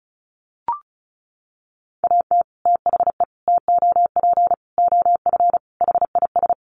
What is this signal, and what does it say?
Morse code, heard as a keyed tone, about 34 wpm, 700 Hz. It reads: AT THE TOP OF HIS
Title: Sets of 5 Words Encore 35wpm